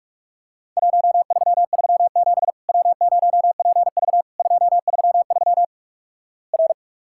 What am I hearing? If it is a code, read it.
Morse code, heard as a keyed tone, about 46 wpm, 700 Hz. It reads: 1337 W0PV 233 R
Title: Worked All Europe DX Contest - QTC Exchange - 45wpm